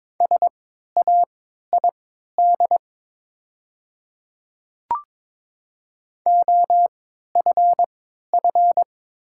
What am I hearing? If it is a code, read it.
Morse code, heard as a keyed tone, about 22 wpm, spaced as for 13 wpm, 700 Hz. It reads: SAID OFF
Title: 3x Character Spacing - Top 200 Words - 22wpm